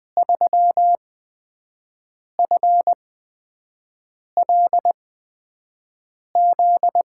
Morse code, heard as a keyed tone, about 20 wpm, spaced as for 5 wpm, 700 Hz. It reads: 3FLZ
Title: USA Call Signs - 6 Characters - 8x Character Spacing - 20wpm